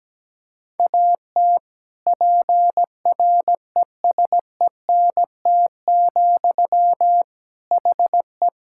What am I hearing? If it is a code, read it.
Morse code, heard as a keyed tone, about 17 wpm, 700 Hz. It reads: AT PRESENT, HE
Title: Sentences from Top 1000 Words - Review Part 5 - 17wpm